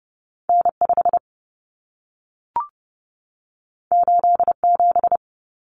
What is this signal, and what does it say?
Morse code, heard as a keyed tone, about 30 wpm, 700 Hz. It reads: N5 87
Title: Common Cut Numbers - 2-Digit 30wpm